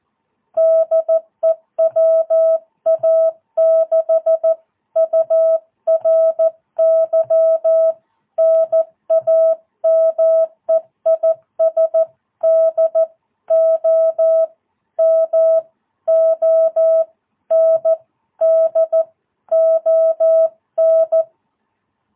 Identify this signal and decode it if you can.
Morse code, heard as a keyed tone, about 14 wpm, spaced as for 12 wpm, 660 Hz. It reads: DE WA6URY NAME IS DOMONDON
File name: jh8cbh / 2022cw